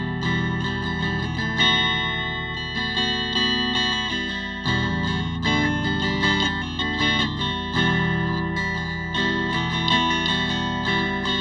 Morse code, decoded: none